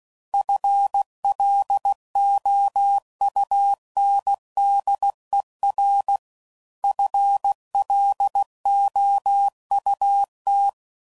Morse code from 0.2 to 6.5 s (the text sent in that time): FLOUNDER